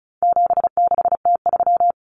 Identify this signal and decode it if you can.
Morse code, heard as a keyed tone, about 34 wpm, 700 Hz. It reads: 76T3
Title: Common Cut Numbers - 4-Digit 35wpm